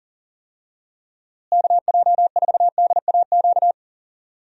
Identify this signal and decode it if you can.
Morse code, heard as a keyed tone, about 40 wpm, 700 Hz. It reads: KJ4DAQ
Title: USA Call Signs - 6 Characters - Repeated 3x - 40wpm